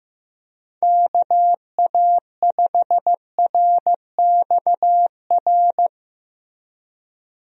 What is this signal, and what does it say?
Morse code, heard as a keyed tone, about 15 wpm, 700 Hz. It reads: KA5RXR
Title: USA Call Signs - 6 Characters - Repeated 3x - 15wpm